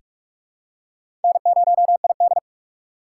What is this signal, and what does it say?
Morse code, heard as a keyed tone, about 46 wpm, 700 Hz. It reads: N0ID
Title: USA Call Signs - 4 Characters - 6x Character Spacing - 45wpm